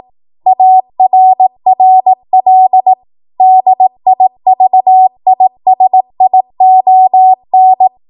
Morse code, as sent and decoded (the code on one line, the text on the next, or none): .- .-. .-. .-.. / -.. .. ...- .. ... .. --- -.
ARRL DIVISION